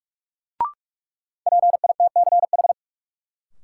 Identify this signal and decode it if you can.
Morse code, heard as a keyed tone, about 46 wpm, 700 Hz. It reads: PITCH
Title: Sentences from Top 1300 Words - New Word Drill Alt - 45wpm